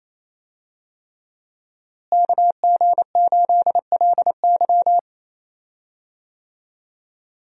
Morse code, heard as a keyed tone, about 27 wpm, 700 Hz. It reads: KG8LY
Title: USA Call Signs - 5 Characters - Repeated 3x - 28wpm